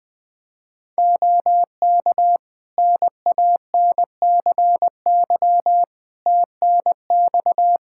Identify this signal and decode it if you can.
Morse code, heard as a keyed tone, about 20 wpm, 700 Hz. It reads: OK NANCY TNX
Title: Practice QSO - Exchange 4 - Ending - 20wpm